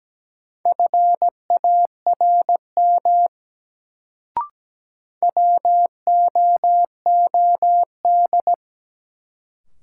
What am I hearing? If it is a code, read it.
Morse code, heard as a keyed tone, about 17 wpm, 700 Hz. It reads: FARM WOOD
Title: Sentences from Top 1000 Words - Review Part 2 - New Word Drill - 17wpm